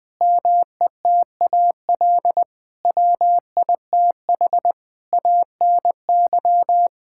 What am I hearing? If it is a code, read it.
Morse code, heard as a keyed tone, about 20 wpm, 700 Hz. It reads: METAL WITH ANY
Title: Sentences from Top 1000 Words - Review Part 6 - 20wpm